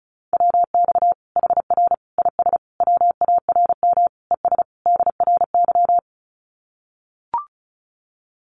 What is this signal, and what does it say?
Morse code, heard as a keyed tone, about 34 wpm, 700 Hz. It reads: WX HR IS WARM ES DRY
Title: Common QSO Phrases 35wpm